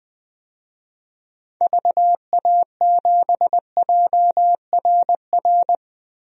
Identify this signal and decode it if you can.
Morse code, heard as a keyed tone, about 20 wpm, 700 Hz. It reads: VA7JRR
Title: DXCC-100 Call Signs - 6 Characters - Repeated 3x - 20wpm